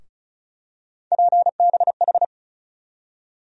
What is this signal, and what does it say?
Morse code, heard as a keyed tone, about 36 wpm, 700 Hz. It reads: PBH
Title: USA Call Sign Postfixes - 3 Letters 35wpm